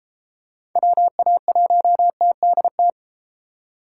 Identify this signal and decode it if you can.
Morse code, heard as a keyed tone, about 32 wpm, 700 Hz. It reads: WA1TDT